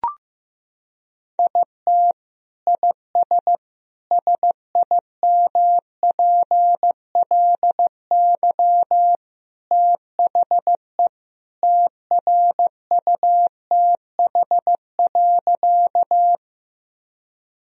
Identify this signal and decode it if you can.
Morse code, heard as a keyed tone, about 15 wpm, 700 Hz. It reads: IT IS SIMPLY THE TRUTH.
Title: Sentences from Top 1000 Words - Review Part 4 - 15wpm